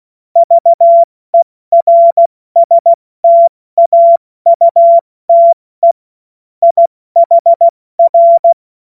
Morse code, heard as a keyed tone, about 16 wpm, spaced as for 14 wpm, 680 Hz. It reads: VERSTAUTE IHR
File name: Morsekurs / Witze / Witz 16 Januar 2024 70 F80